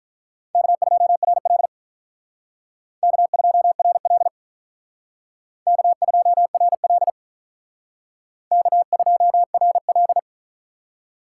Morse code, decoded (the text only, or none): K2RL K2RL K2RL K2RL